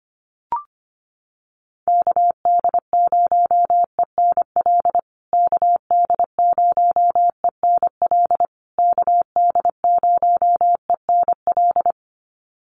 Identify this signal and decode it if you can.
Morse code, heard as a keyed tone, about 25 wpm, 700 Hz. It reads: KD0ENL KD0ENL KD0ENL
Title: USA Call Signs Repeated 3x 25wpm